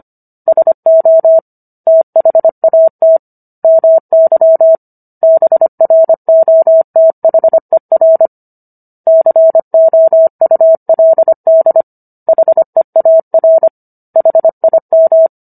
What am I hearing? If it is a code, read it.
Morse code, heard as a keyed tone, about 25 wpm, 650 Hz. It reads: SO THAT MY BROTHER COULD HEAR HIM